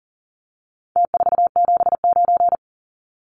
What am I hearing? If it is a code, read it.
Morse code, heard as a keyed tone, about 40 wpm, 700 Hz. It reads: T479
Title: Numbers - Contest Style - 4-Digit 40wpm